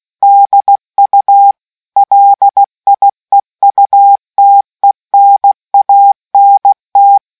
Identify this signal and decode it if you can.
Morse code, heard as a keyed tone, about 16 wpm, 800 Hz. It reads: DU LIEUTENANT